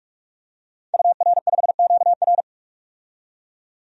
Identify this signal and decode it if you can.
Morse code, heard as a keyed tone, about 41 wpm, 700 Hz. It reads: UR5QR
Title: DXCC-100 Call Signs - 5 Characters - 45wpm